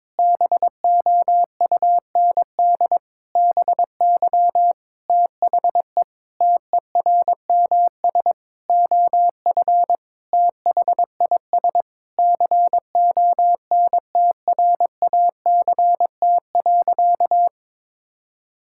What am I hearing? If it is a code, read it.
Morse code, heard as a keyed tone, about 22 wpm, 700 Hz. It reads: BOUND BY THE TERMS OF THIS CONTRACT.
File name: Sentences with 9-Word Phrases - 22wpm